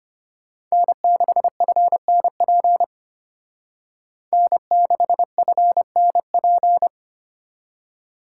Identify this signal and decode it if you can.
Morse code, heard as a keyed tone, about 33 wpm, 700 Hz. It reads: N6FNP N6FNP